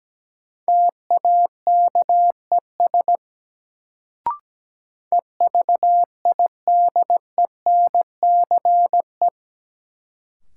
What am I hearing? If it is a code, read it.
Morse code, heard as a keyed tone, about 17 wpm, 700 Hz. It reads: TAKES EVIDENCE
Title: Sets of 5 Words Encore - New Word Drill - 17wpm